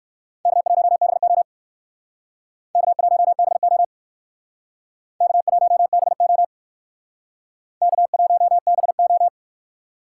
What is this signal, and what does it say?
Morse code, heard as a keyed tone, about 62 wpm, 700 Hz. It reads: K1BO K1BO K1BO K1BO